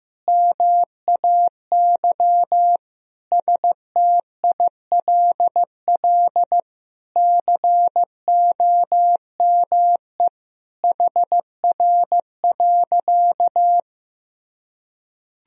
Morse code, heard as a keyed tone, about 15 wpm, 700 Hz. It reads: MAY STILL COME HR.
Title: Sentences from Top 300 Words and Abbreviations 15wpm